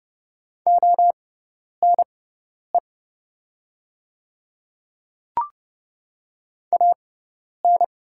Morse code, heard as a keyed tone, about 30 wpm, spaced as for 10 wpm, 700 Hz. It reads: ONE AN
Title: 6x Character Spacing - Top 100 Words - 30wpm